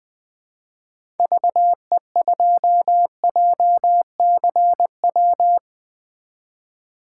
Morse code, heard as a keyed tone, about 20 wpm, 700 Hz. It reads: VE2JCW
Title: 2023 SOTA Chaser Call Signs - 20wpm